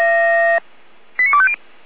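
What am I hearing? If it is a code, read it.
Five-tone selective call (ZVEI), heard as quick runs of five tones, tones of about 70 ms. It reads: 89250
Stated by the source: ZVEI five-tone selective call, address 89294, received on an SDR.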